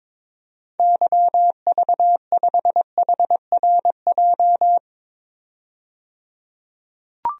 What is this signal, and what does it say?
Morse code, heard as a keyed tone, about 22 wpm, 700 Hz. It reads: YV5HRJ
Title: DXCC-100 Call Signs - 6 Characters - 22wpm